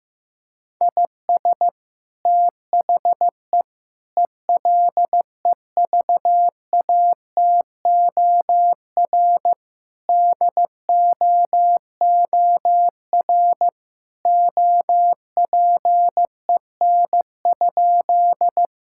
Morse code, heard as a keyed tone, about 15 wpm, 700 Hz. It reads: IS THE ELEVATOR DOOR OPEN?
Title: Sentences from Top 1600 Words 15wpm